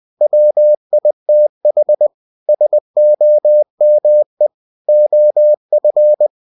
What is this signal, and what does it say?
Morse code, heard as a keyed tone, about 20 wpm, 600 Hz. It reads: WITH SOME OF
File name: cw-books / winnie-the-pooh / winnie-20wpm-page-0088